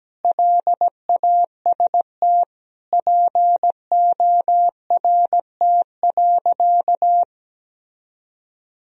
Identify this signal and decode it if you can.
Morse code, heard as a keyed tone, about 17 wpm, 700 Hz. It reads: LAST PORT.
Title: Sentences from Top 1000 Words - Review Part 1 - 17wpm